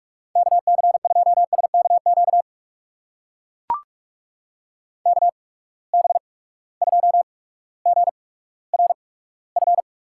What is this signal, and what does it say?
Morse code, heard as a keyed tone, about 43 wpm, spaced as for 19 wpm, 700 Hz. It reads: KC2SKQ KB2GRF
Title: USA Call Signs - 6 Characters - 8x Character Spacing - 45wpm